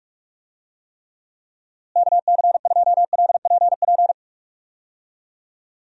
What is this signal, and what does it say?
Morse code, heard as a keyed tone, about 41 wpm, 700 Hz. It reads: KC2LPP
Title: USA Call Signs - 6 Characters - Repeated 3x - 45wpm